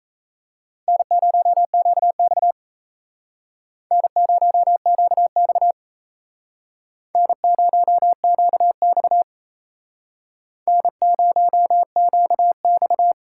Morse code, heard as a keyed tone, about 35 wpm, 700 Hz. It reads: N0QX N0QX N0QX N0QX
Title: Speed Racing - USA 1x2 Call Signs 28wpm